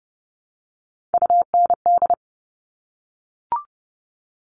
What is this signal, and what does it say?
Morse code, heard as a keyed tone, about 30 wpm, 700 Hz. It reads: UND
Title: All Cut Numbers - 3-Digit 30wpm